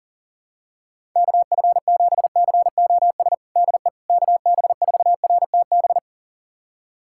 Morse code, heard as a keyed tone, about 40 wpm, 700 Hz. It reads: KF7COS DE KB4RTB
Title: USA Callsign DE Callsign - 6 Characters - 40wpm